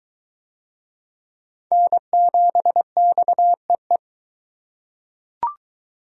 Morse code, heard as a keyed tone, about 20 wpm, 700 Hz. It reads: N7XEE